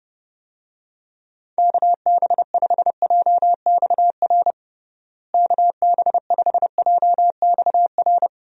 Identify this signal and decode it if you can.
Morse code, heard as a keyed tone, about 30 wpm, 700 Hz. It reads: KB5JXR KB5JXR
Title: USA Call Signs - 6 Characters - Repeated 3x - 30wpm